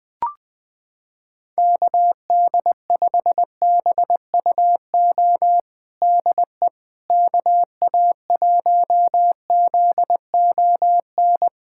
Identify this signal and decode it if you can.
Morse code, heard as a keyed tone, about 20 wpm, 700 Hz. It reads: KD5BUO DE KA1ZON
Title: USA Callsign DE Callsign - 6 Characters - 20wpm